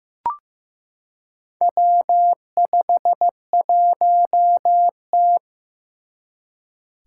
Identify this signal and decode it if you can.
Morse code, heard as a keyed tone, about 15 wpm, 700 Hz. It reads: W51T
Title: Four Challenging Letter-Number Combinations 15wpm